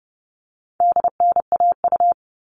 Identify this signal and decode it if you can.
Morse code, heard as a keyed tone, about 30 wpm, 700 Hz. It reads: DNAU
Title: All Cut Numbers - 4-Digit 30wpm